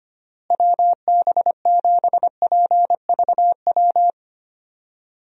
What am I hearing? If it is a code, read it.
Morse code, heard as a keyed tone, about 25 wpm, 700 Hz. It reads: WB7PVW
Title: USA Call Signs - 6 Characters - Repeated 2x - 25wpm